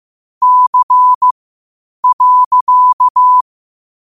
Morse code, heard as a keyed tone, about 15 wpm, spaced as for 8 wpm, 1000 Hz. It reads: C.
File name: qotdfiles / 8wpm